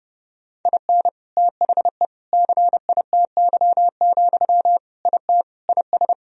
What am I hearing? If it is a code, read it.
Morse code, heard as a keyed tone, about 32 wpm, 700 Hz. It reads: IN THE CITY, IT IS
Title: Sentences from Top 1000 Words - Review Part 3 - 30wpm